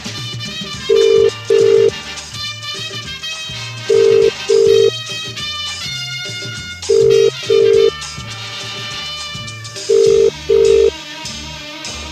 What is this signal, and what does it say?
Ringback tone (UK), 400 + 450 Hz.